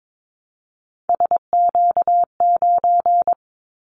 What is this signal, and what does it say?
Morse code, heard as a keyed tone, about 22 wpm, 700 Hz. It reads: SQ9
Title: Callsign Prefixes for Top 25 DXCC - 22wpm